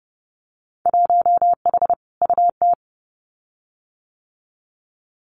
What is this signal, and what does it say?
Morse code, heard as a keyed tone, about 31 wpm, 700 Hz. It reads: 1H UT
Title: Winter Field Day Exchanges 30wpm